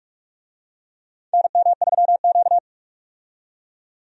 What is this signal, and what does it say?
Morse code, heard as a keyed tone, about 47 wpm, 700 Hz. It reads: NM3Q
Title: USA Call Signs - 4 Characters - 45wpm